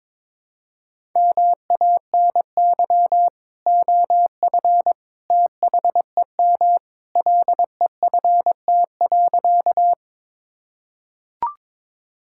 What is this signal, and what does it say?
Morse code, heard as a keyed tone, about 22 wpm, 700 Hz. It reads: MANY OF THEM LEFT.
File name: practice / 22wpm / Sentences with 2-Word Phrases - 22wpm